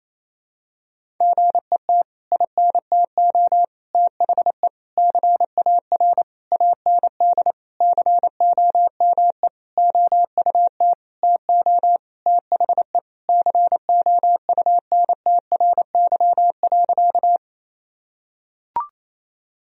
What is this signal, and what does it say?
Morse code, heard as a keyed tone, about 28 wpm, 700 Hz. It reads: GET INTO THE CAR AND COME OUT TO THE COUNTRY.
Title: Sentences from Top 1000 Words - Review Part 1 - 28wpm